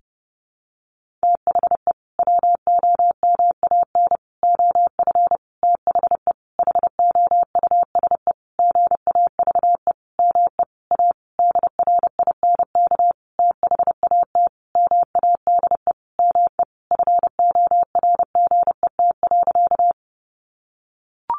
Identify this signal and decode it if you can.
Morse code, heard as a keyed tone, about 30 wpm, 700 Hz. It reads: THE WOMAN OF THE HOUSE GAVE ME A DRINK THAT MADE ME FORGET.
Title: Sentences from Top 1500 Words 30wpm